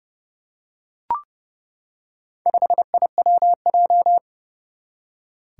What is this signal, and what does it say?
Morse code, heard as a keyed tone, about 32 wpm, 700 Hz. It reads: 5IWJ
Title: Four Challenging Letter-Number Combinations 30wpm